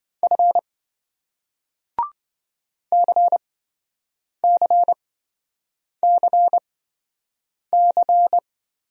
Morse code, heard as a keyed tone, about 27 wpm, spaced as for 7 wpm, 700 Hz. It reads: F CCCC